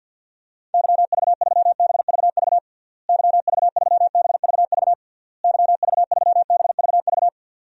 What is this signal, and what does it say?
Morse code, heard as a keyed tone, about 51 wpm, 700 Hz. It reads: YV3BVV YV3BVV YV3BVV
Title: DXCC-100 Call Signs - 6 Characters - Repeated 3x - 50wpm